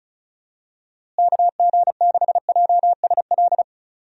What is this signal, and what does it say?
Morse code, heard as a keyed tone, about 36 wpm, 700 Hz. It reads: KG6JSL
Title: USA Call Signs - 6 Characters - Repeated 3x - 35wpm